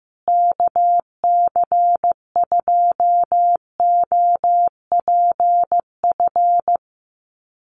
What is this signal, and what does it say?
Morse code, heard as a keyed tone, about 15 wpm, 700 Hz. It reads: KC2OPF
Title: USA Call Signs Repeated 2x 15wpm